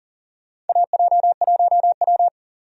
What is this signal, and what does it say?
Morse code, heard as a keyed tone, about 40 wpm, 700 Hz. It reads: AJ1W
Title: USA Call Signs - 4 Characters - 40wpm